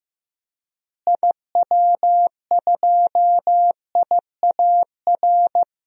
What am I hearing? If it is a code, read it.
Morse code, heard as a keyed tone, about 15 wpm, 700 Hz. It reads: IW2IAR
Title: DXCC-100 Call Signs - 6 Characters - Repeated 2x - 15wpm